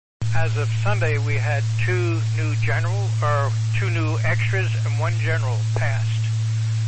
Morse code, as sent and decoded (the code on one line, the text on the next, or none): none